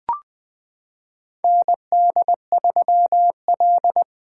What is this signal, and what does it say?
Morse code, heard as a keyed tone, about 20 wpm, 700 Hz. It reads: ND3L
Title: K1USN SST QSO Practice - 20wpm